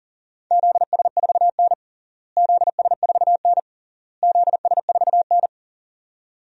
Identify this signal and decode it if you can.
Morse code, heard as a keyed tone, about 40 wpm, 700 Hz. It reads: ZS4N ZS4N ZS4N
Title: DXCC-100 Call Signs - 4 Characters - Repeated 3x - 40wpm